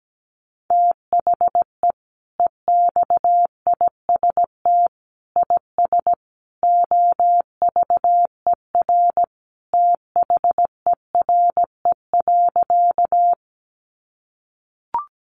Morse code, heard as a keyed tone, about 17 wpm, 700 Hz. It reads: THE EXIST IS OVER THERE.
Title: Sentences from Top 1100 Words 17wpm